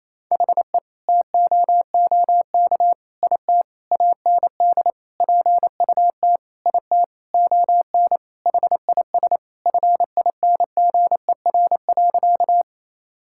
Morse code, heard as a keyed tone, about 28 wpm, 700 Hz. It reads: HE TOOK IT AND PUT IT ON HIS FINGER.
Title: Sentences from Top 1600 Words 28wpm